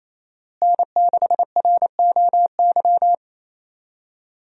Morse code, heard as a keyed tone, about 28 wpm, 700 Hz. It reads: N6ROY
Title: USA Call Signs - 5 Characters - Repeated 3x - 28wpm